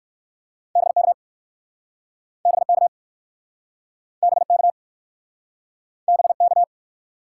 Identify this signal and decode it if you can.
Morse code, heard as a keyed tone, about 61 wpm, 700 Hz. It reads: BK BK BK BK